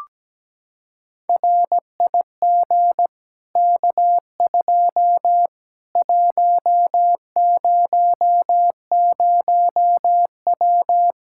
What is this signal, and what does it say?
Morse code, heard as a keyed tone, about 17 wpm, 700 Hz. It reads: RIG K2 100W
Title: Practice QSO - Exchange 2 - Info - 17wpm